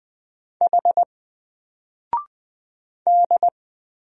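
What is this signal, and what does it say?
Morse code, heard as a keyed tone, about 20 wpm, 700 Hz. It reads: H D
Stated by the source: Single letter and number Morse drill at 20 wpm, from N4LCD.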